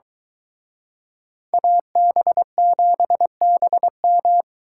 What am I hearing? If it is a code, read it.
Morse code, heard as a keyed tone, about 23 wpm, 700 Hz. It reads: AB7BM